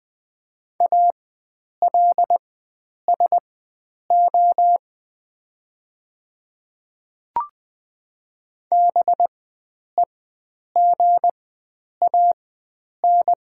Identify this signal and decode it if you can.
Morse code, heard as a keyed tone, about 20 wpm, spaced as for 9 wpm, 690 Hz. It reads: ALSO BEGAN
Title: 4x Character Spacing - Top 300 Words - 20wpm